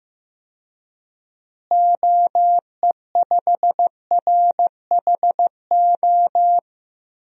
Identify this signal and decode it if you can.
Morse code, heard as a keyed tone, about 15 wpm, 700 Hz. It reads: OE5RHO
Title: DXCC-100 Call Signs - 6 Characters - Repeated 3x - 15wpm